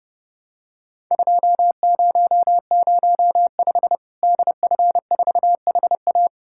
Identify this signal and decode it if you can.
Morse code, heard as a keyed tone, about 30 wpm, 700 Hz. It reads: 2005 DF4HA
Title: Worked All Europe DX Contest - QTC Exchange - 30wpm